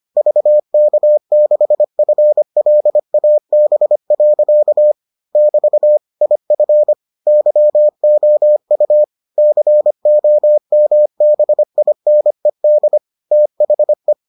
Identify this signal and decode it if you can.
Morse code, heard as a keyed tone, about 25 wpm, 600 Hz. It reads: VK6FLAB. <BT> IF YOU COMBINED THE